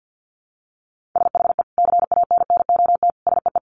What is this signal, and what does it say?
Morse code, heard as a keyed tone, about 50 wpm, 700 Hz. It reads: SHE CANNOT SEE
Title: Additional Easy Phrases 50wpm